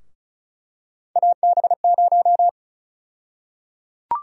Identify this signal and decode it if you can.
Morse code, heard as a keyed tone, about 36 wpm, 700 Hz. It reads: AB0
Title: USA Call Sign Prefixes - Repeated 2x - 35wpm